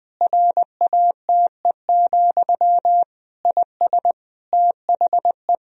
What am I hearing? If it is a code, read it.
Morse code, heard as a keyed tone, about 20 wpm, 700 Hz. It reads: RATE, IS THE
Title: Sentences from Top 1000 Words - Review Part 4 - 20wpm